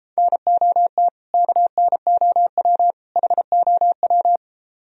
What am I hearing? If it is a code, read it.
Morse code, heard as a keyed tone, about 33 wpm, 700 Hz. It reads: NOT KNOW HOW